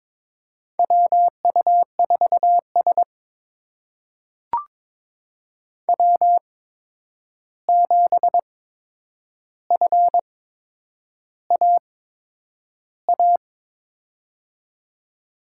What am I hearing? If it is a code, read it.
Morse code, heard as a keyed tone, about 22 wpm, spaced as for 9 wpm, 700 Hz. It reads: WU4S W7FAA
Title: USA Call Signs - 8x Character Spacing - 22wpm